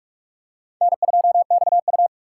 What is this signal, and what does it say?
Morse code, heard as a keyed tone, about 43 wpm, 700 Hz. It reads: N2XU